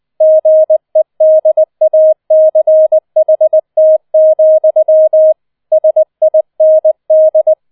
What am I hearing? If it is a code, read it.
Morse code, heard as a keyed tone, about 20 wpm, 630 Hz. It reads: GEDACHT, SIND